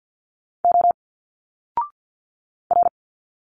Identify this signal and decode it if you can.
Morse code, heard as a keyed tone, about 50 wpm, 700 Hz. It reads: O R
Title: Single Letters 50wpm